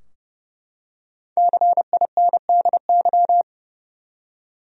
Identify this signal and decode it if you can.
Morse code, heard as a keyed tone, about 32 wpm, 700 Hz. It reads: CINDY
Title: USA Common Names - Max 5 Letters - 30wpm